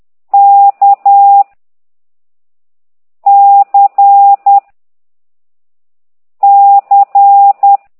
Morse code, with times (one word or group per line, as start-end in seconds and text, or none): 0.3-1.4 K
3.3-4.6 C
6.4-7.8 C